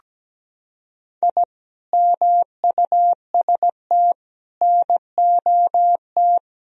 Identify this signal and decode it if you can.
Morse code, heard as a keyed tone, about 17 wpm, 700 Hz. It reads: I MUST NOT
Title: Sentences from Top 1000 Words - Review Part 2 - 17wpm